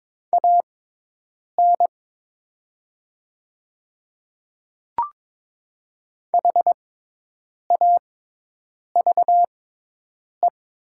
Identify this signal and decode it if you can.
Morse code, heard as a keyed tone, about 22 wpm, spaced as for 8 wpm, 690 Hz. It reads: AN HAVE
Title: 6x Character Spacing - Top 100 Words - 22wpm